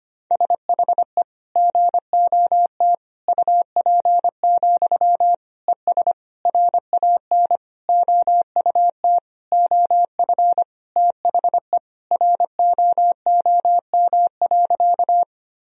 Morse code, heard as a keyed tone, about 25 wpm, 700 Hz. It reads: SHE GOT UP, ES RAN OUT OF THE ROOM.
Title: Sentences from Top 500 Words and Abbreviations 25wpm